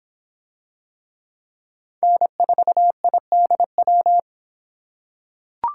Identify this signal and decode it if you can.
Morse code, heard as a keyed tone, about 26 wpm, 700 Hz. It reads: N4IDW